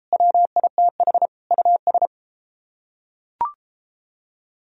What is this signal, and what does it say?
Morse code, heard as a keyed tone, about 33 wpm, 700 Hz. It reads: WITH US